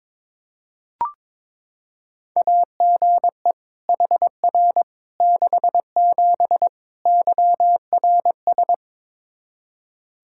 Morse code, heard as a keyed tone, about 22 wpm, 700 Hz. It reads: AGE HR 67 YRS
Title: K1USN SST QSO Practice 22wpm